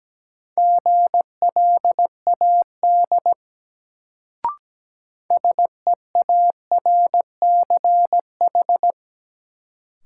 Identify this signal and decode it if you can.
Morse code, heard as a keyed tone, about 17 wpm, 700 Hz. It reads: GLAD SEARCH
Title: Sentences from Top 800 Words - New Word Drill - 17wpm